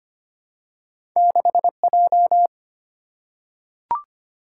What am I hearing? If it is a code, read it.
Morse code, heard as a keyed tone, about 25 wpm, 690 Hz. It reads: 6J
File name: Two Letter-Number Combinations - 25wpm